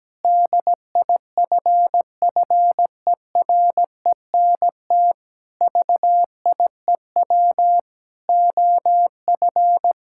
Morse code, heard as a keyed tone, about 17 wpm, 700 Hz. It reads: DIFFERENT VIEW OF
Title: Sentences from Top 1600 Words 17wpm